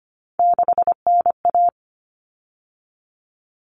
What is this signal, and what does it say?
Morse code, heard as a keyed tone, about 25 wpm, 700 Hz. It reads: 6NA
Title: Common Cut Numbers - 3-Digit 25wpm